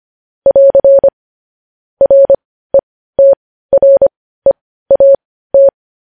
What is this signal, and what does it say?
Morse code, heard as a keyed tone, about 25 wpm, spaced as for 15 wpm, 550 Hz. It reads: <AR> RETREAT
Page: news headlines in morse code at 15 wpm